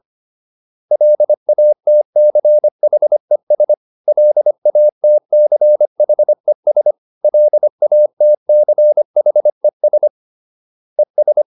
Morse code, heard as a keyed tone, about 25 wpm, 600 Hz. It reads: LATCHES LATCHES LATCHES ES